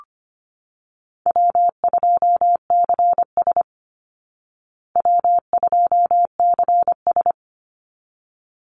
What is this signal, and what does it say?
Morse code, heard as a keyed tone, about 25 wpm, 700 Hz. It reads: W2CS W2CS
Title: Call Signs Repeated 3x - 25wpm